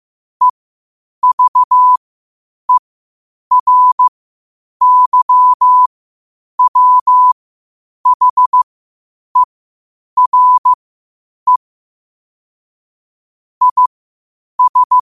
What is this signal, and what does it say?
Morse code, heard as a keyed tone, about 15 wpm, spaced as for 8 wpm, 1000 Hz. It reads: EVERYWHERE IS
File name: qotdfiles / 8wpm